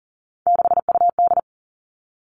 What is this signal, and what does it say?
Morse code, heard as a keyed tone, about 38 wpm, 700 Hz. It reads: 6UD
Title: Three Challenging Letter-Number Combinations 40wpm